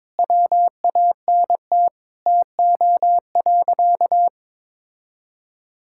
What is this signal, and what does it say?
Morse code, heard as a keyed tone, about 22 wpm, 700 Hz. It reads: WANT TO.
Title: Sentences from Top 1000 Words - Review Part 5 - 22wpm